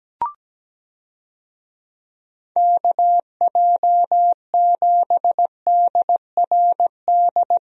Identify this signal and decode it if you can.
Morse code, heard as a keyed tone, about 17 wpm, 700 Hz. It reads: KJ7DRD
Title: USA Call Signs - 6 Characters - Repeated 2x - 17wpm